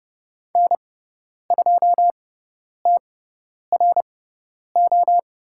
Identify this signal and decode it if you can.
Morse code, heard as a keyed tone, about 30 wpm, spaced as for 10 wpm, 700 Hz. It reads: N2TRO
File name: Lesson 16g - Letter c - Call Signs 30wpm